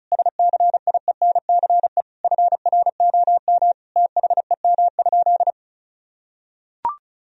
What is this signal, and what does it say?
Morse code, heard as a keyed tone, about 36 wpm, 700 Hz. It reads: SCIENCE FROM THEM?
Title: Sentences from Top 1000 Words - Review Part 2 - 35wpm